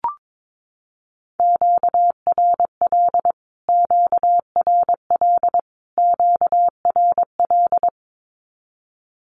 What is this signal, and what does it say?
Morse code, heard as a keyed tone, about 22 wpm, 700 Hz. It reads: QRL QRL QRL
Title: Q-Codes Repeated 3x 22wpm